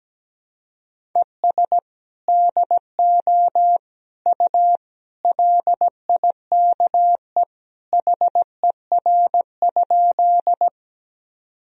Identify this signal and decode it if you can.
Morse code, heard as a keyed tone, about 17 wpm, 700 Hz. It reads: ES DO U LIKE HER?
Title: Sentences from Top 100 Words and Abbreviations 17wpm